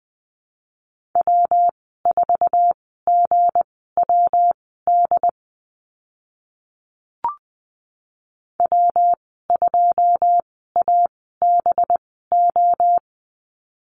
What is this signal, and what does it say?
Morse code, heard as a keyed tone, about 20 wpm, spaced as for 14 wpm, 700 Hz. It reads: W4GWD W2ABO
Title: 193 Lesson 31g - Letter v - Call Signs 20wpm